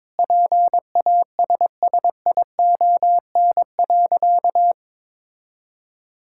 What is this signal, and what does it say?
Morse code, heard as a keyed tone, about 22 wpm, 700 Hz. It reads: PASSION.
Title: Sentences from Top 1600 Words 22wpm